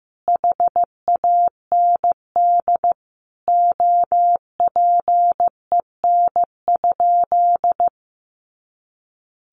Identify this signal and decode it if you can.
Morse code, heard as a keyed tone, about 15 wpm, 700 Hz. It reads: HAND OPEN?
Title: Sentences from Top 300 Words 15wpm